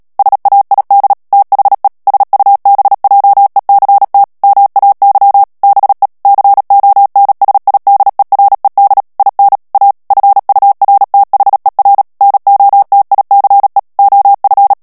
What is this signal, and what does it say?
Morse code, heard as a keyed tone, about 37 wpm, 800 Hz. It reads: SAID THE SUBJECT MAY BE CONSIDERED IN A FURTHER NOTICE OF
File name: morse mp3 / ANGLAIS texte clair 3 vitesse 35 mots minute